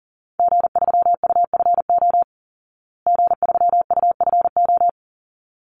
Morse code, heard as a keyed tone, about 40 wpm, 700 Hz. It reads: G3UFO G3UFO
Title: Call Signs Repeated 3x - 40wpm